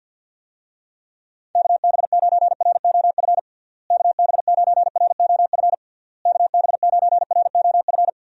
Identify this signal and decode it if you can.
Morse code, heard as a keyed tone, about 53 wpm, 700 Hz. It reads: KB9ROF KB9ROF KB9ROF
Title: USA Call Signs - 6 Characters - Repeated 3x - 50wpm